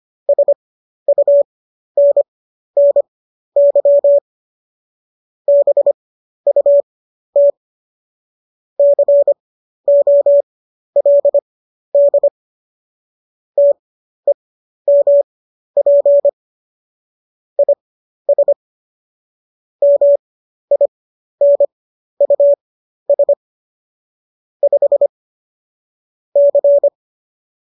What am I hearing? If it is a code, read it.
Morse code, heard as a keyed tone, about 25 wpm, spaced as for 12 wpm, 570 Hz. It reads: SUNNY BUT COLD TEMP IS MINUS 5 C